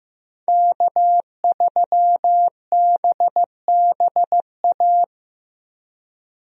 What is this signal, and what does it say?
Morse code, heard as a keyed tone, about 15 wpm, 700 Hz. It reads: K3BBA
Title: USA Call Signs - 5 Characters - Repeated 3x - 15wpm